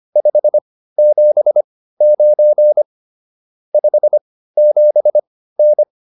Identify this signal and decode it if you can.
Morse code, heard as a keyed tone, about 25 wpm, spaced as for 15 wpm, 600 Hz. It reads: 579 57N